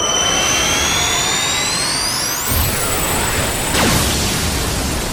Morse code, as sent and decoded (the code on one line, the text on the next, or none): none